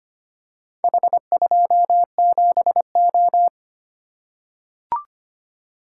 Morse code, heard as a keyed tone, about 25 wpm, 700 Hz. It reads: H27O
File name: practice / 25wpm / Four Letter-Number Combinations - 25wpm